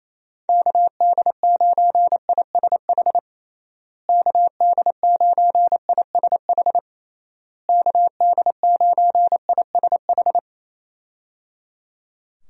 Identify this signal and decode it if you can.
Morse code, heard as a keyed tone, about 28 wpm, 700 Hz. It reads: KD9ISH KD9ISH KD9ISH
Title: USA Call Signs - 6 Characters - Repeated 3x - 28wpm